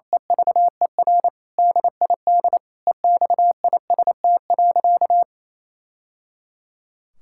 Morse code, heard as a keyed tone, about 28 wpm, 700 Hz. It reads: EVER DID EXIST.
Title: Sentences from Top 1000 Words - Review Part 5 - 28wpm